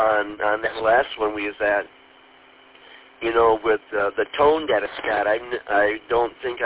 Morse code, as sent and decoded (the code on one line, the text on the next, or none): none